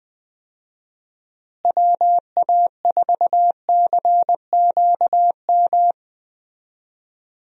Morse code, heard as a keyed tone, about 20 wpm, 700 Hz. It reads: WA4CQM
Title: USA Call Signs - 6 Characters - Repeated 2x - 20wpm